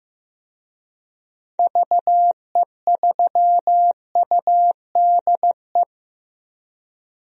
Morse code, heard as a keyed tone, about 15 wpm, 700 Hz. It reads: VE3UDE
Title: DXCC-100 Call Signs - 6 Characters - 15wpm